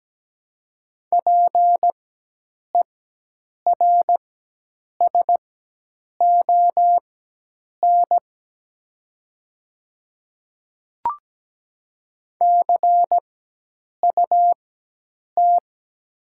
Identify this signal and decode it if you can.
Morse code, heard as a keyed tone, about 17 wpm, spaced as for 8 wpm, 700 Hz. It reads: PERSON CUT
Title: 4x Character Spacing - Top 500 Words - 17wpm